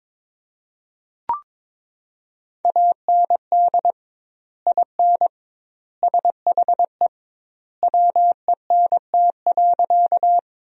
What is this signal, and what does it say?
Morse code, heard as a keyed tone, about 22 wpm, 700 Hz. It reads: AND IN SHE WENT.
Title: 2x Word-Spacing - Sentences from Top 200 Words - 22wpm